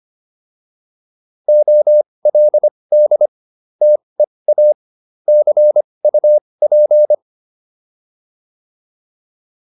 Morse code, heard as a keyed tone, about 25 wpm, spaced as for 20 wpm, 600 Hz. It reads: OLD TEA CUP